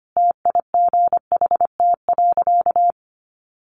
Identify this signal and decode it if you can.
Morse code, heard as a keyed tone, about 25 wpm, 700 Hz. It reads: TIGHT.
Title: A Sound Of Thunder 25wpm